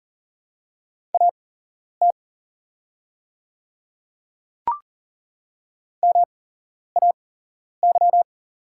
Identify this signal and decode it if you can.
Morse code, heard as a keyed tone, about 40 wpm, spaced as for 11 wpm, 700 Hz. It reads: AT MAY
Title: 8x Character Spacing - Top 100 Words - 40wpm